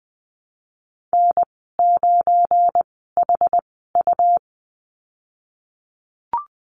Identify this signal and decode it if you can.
Morse code, heard as a keyed tone, about 20 wpm, spaced as for 15 wpm, 700 Hz. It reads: N9HU
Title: 186 Lesson 30g - Letter b - Call Signs 20wpm